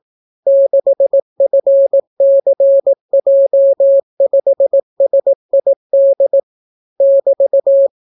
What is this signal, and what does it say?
Morse code, heard as a keyed tone, about 18 wpm, 550 Hz. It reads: 6FCJ5SID <BT>